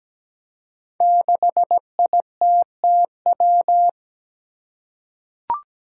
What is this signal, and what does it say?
Morse code, heard as a keyed tone, about 17 wpm, 700 Hz. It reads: 6ITTW
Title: Five Challenging Letter-Number Combinations 17wpm